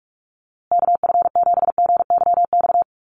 Morse code, heard as a keyed tone, about 44 wpm, 700 Hz. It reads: KF7GYX
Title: USA Call Signs Repeated 3x 45wpm